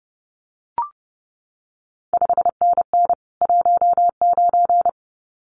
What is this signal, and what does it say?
Morse code, heard as a keyed tone, about 30 wpm, 700 Hz. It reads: 5NN 19
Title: CQ WW DX Contest Exchange Encore - 30wpm